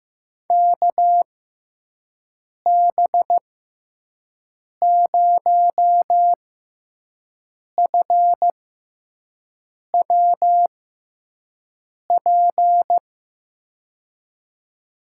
Morse code, heard as a keyed tone, about 15 wpm, spaced as for 5 wpm, 700 Hz. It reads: KB0FWP
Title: USA Call Signs - 6 Characters - 6x Character Spacing - 15wpm